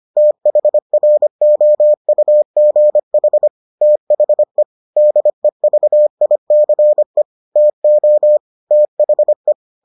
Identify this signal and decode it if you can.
Morse code, heard as a keyed tone, about 25 wpm, 600 Hz. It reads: THROUGH THE DEVICE TO THE